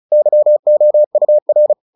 Morse code, heard as a keyed tone, about 34 wpm, 600 Hz. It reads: YOUR
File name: the old man and the sea 2 35wpm